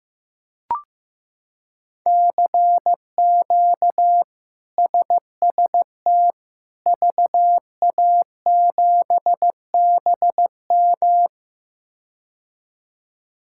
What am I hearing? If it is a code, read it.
Morse code, heard as a keyed tone, about 15 wpm, 700 Hz. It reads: CQ SST VA7BM
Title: K1USN SST QSO Practice - 15wpm